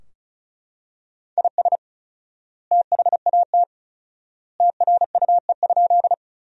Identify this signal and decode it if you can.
Morse code, heard as a keyed tone, about 36 wpm, 700 Hz. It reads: IS THAT TRUE?
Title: 4x Word-Spacing - Sentences from Top 400 Words - 35wpm